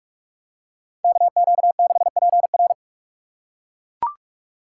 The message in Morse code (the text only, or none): KQ6PR